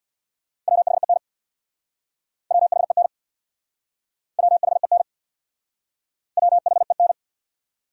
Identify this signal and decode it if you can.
Morse code, heard as a keyed tone, about 66 wpm, 700 Hz. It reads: WHEN WHEN WHEN WHEN